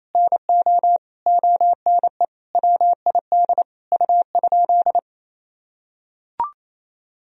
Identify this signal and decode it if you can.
Morse code, heard as a keyed tone, about 28 wpm, 700 Hz. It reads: NO ONE WID U?
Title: Sentences from Top 500 Words and Abbreviations 28wpm